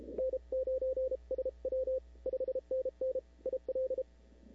Morse code, heard as a keyed tone, about 31 wpm, 440 Hz. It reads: N9SW 5NN IL
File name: CW / 2005-k5tr arrl cw dx -03:00